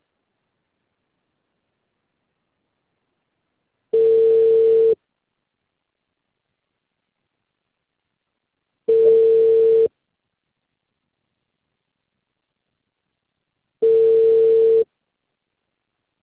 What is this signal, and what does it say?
Ringback tone (China, 450 Hz).